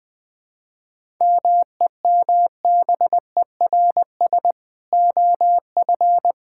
Morse code, heard as a keyed tone, about 20 wpm, 700 Hz. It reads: MEMBERS OF
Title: Sentences with 3-Word Phrases - 20wpm